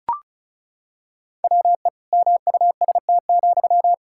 Morse code, heard as a keyed tone, about 36 wpm, 700 Hz. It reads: WE MUST,
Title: Sentences from Top 1600 Words 35wpm